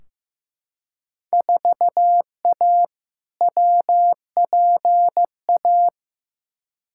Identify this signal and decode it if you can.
Morse code, heard as a keyed tone, about 15 wpm, 700 Hz. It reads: 4A WPA
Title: Field Day Exchanges Encore - 15wpm